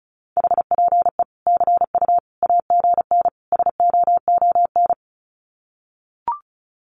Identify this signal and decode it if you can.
Morse code, heard as a keyed tone, about 34 wpm, 700 Hz. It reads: HPE CU AGN SOON
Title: Common QSO Phrases 35wpm